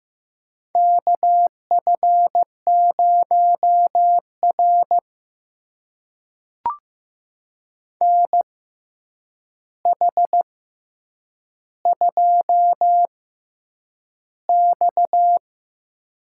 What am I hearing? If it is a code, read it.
Morse code, heard as a keyed tone, about 15 wpm, spaced as for 8 wpm, 700 Hz. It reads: KF0R NH2X
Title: USA Call Signs - 4 Characters - 6x Character Spacing - 15wpm